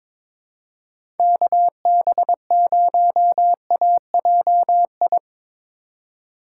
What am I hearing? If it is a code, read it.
Morse code, heard as a keyed tone, about 21 wpm, 700 Hz. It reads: KB0AJI
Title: USA Call Signs - 6 Characters - 3x Character Spacing - 22wpm